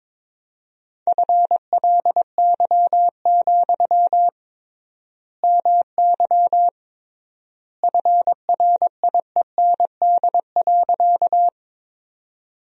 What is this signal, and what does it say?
Morse code, heard as a keyed tone, about 22 wpm, 700 Hz. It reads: FLY, MY FRIEND.
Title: 3x Word-Spacing - Sentences from Top 500 Words - 22wpm